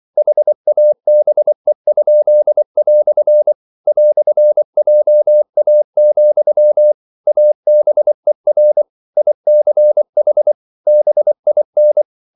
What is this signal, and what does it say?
Morse code, heard as a keyed tone, about 24 wpm, 600 Hz. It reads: HABE?" "JA, ABER ICH BIN